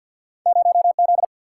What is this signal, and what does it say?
Morse code, heard as a keyed tone, about 48 wpm, 700 Hz. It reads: 0 Z
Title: Single-Letter Single-Number 50wpm